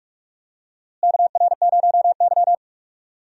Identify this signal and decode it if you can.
Morse code, heard as a keyed tone, about 45 wpm, 700 Hz. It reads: KR0Y
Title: USA Call Signs - 4 Characters - Repeated 2x - 45wpm